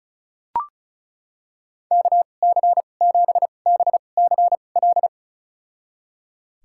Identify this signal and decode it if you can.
Morse code, heard as a keyed tone, about 36 wpm, spaced as for 25 wpm, 700 Hz. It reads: KC7BCL
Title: USA Call Signs - 6 Characters - 2x Character Spacing - 35wpm